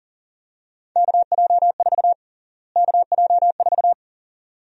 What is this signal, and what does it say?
Morse code, heard as a keyed tone, about 40 wpm, 700 Hz. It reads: KJ4 KJ4
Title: USA Call Sign Prefixes - Repeated 2x - 40wpm